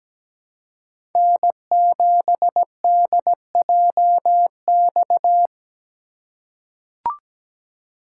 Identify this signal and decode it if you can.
Morse code, heard as a keyed tone, about 17 wpm, 700 Hz. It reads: N7DJX
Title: USA Call Signs - 5 Characters - 4x Character Spacing - 17wpm